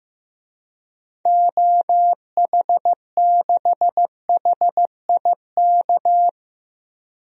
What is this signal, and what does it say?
Morse code, heard as a keyed tone, about 15 wpm, 700 Hz. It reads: OH6HIK
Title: DXCC-100 Call Signs - 6 Characters - 15wpm